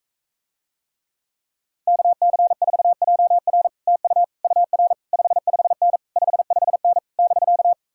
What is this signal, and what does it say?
Morse code, heard as a keyed tone, about 44 wpm, 700 Hz. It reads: KC4JR TU UR 55N 55N <BK>